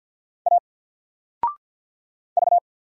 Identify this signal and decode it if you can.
Morse code, heard as a keyed tone, about 48 wpm, 700 Hz. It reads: A V
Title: Single Letter-Number Alt 50wpm